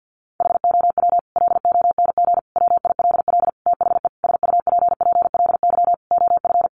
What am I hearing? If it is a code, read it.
Morse code, heard as a keyed tone, about 48 wpm, 700 Hz. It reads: HOW LONG WILL THE SUPPLY OF
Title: Sentences from Top 1200 Words 50wpm